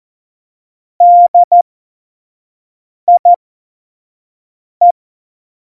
Morse code, heard as a keyed tone, about 14 wpm, spaced as for 5 wpm, 700 Hz. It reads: DIE
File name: Morsekurs / Witze / Witz 01. September 2020, 25 F70